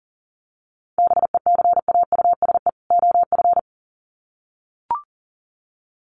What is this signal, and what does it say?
Morse code, heard as a keyed tone, about 40 wpm, 700 Hz. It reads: BECAUSE OF
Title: Sets of 2 Words 40wpm